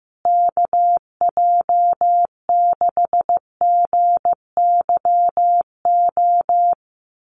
Morse code, heard as a keyed tone, about 14 wpm, 700 Hz. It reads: KJ6GYO
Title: USA Call Signs Repeated 2x 15wpm